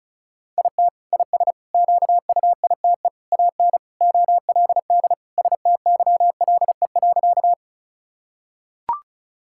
Morse code, heard as a keyed tone, about 36 wpm, 700 Hz. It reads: IT IS QUITE AN OLD STYLE.
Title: Sentences from Top 1000 Words - Review Part 5 - 35wpm